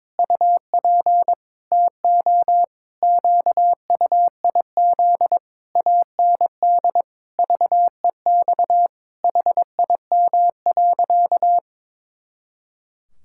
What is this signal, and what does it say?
Morse code, heard as a keyed tone, about 22 wpm, 700 Hz. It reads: UP TO QUIZ AND VEX HIM.
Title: Pangram Challenge v2 - 22wpm